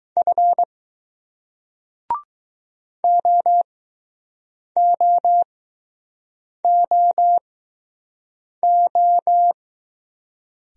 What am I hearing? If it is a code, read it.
Morse code, heard as a keyed tone, about 20 wpm, spaced as for 6 wpm, 700 Hz. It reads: F OOOO